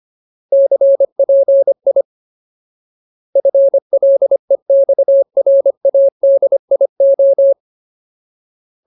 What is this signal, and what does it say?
Morse code, heard as a keyed tone, about 25 wpm, 550 Hz. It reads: CPI FLEXRADIO
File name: Ultimate Word List / 25WPM Ultimate